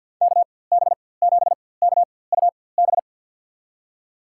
Morse code, heard as a keyed tone, about 52 wpm, spaced as for 23 wpm, 700 Hz. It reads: KB7KUB
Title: USA Call Signs - 6 Characters - 4x Character Spacing - 50wpm